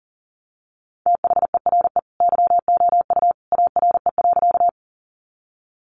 Morse code, heard as a keyed tone, about 40 wpm, 700 Hz. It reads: THERE YOU ARE.
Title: Sentences from Top 100 Words 40wpm